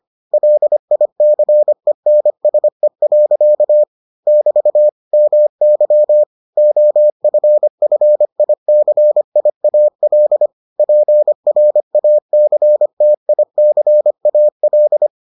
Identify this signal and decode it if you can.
Morse code, heard as a keyed tone, about 25 wpm, 600 Hz. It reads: LICENSE. <BT> MY OFFICIAL PRACTICAL